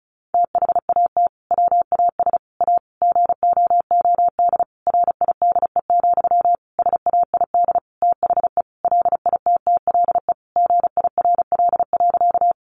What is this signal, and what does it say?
Morse code, heard as a keyed tone, about 34 wpm, 700 Hz. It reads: THAT WAS A GOOD RIDE, SAID THE LITTLE GIRL.
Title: Sentences from Top 1200 Words 35wpm